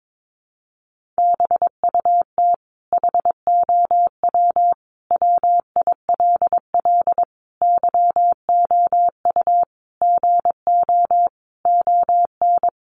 Morse code, heard as a keyed tone, about 22 wpm, 700 Hz. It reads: BUT HOW WILL YOU GO ON
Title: Sentences from Top 1100 Words 22wpm